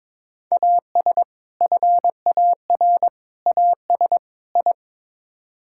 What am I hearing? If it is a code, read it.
Morse code, heard as a keyed tone, about 22 wpm, 700 Hz. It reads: AS FAR AS I
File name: practice / 22wpm / Build A Phrase - 5-Word Phrases - 22wpm